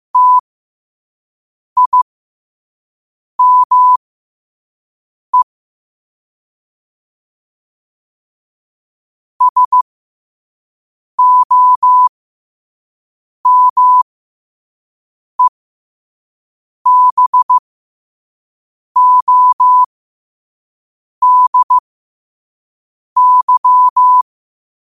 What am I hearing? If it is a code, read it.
Morse code, heard as a keyed tone, about 15 wpm, spaced as for 5 wpm, 1000 Hz. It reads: TIME SOMEBODY
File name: qotdfiles / 5wpm